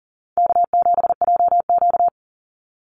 Morse code, heard as a keyed tone, about 40 wpm, 700 Hz. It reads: K7JQ
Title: Call Signs Repeated 3x - 40wpm